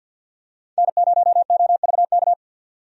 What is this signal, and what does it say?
Morse code, heard as a keyed tone, about 48 wpm, 700 Hz. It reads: N0OVK
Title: USA Call Signs - 5 Characters - 50wpm